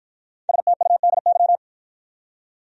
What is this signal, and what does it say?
Morse code, heard as a keyed tone, about 53 wpm, 700 Hz. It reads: STUDY